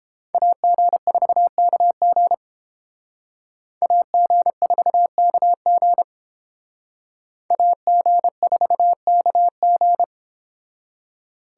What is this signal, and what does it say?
Morse code, heard as a keyed tone, about 30 wpm, 700 Hz. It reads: AG4KG AG4KG AG4KG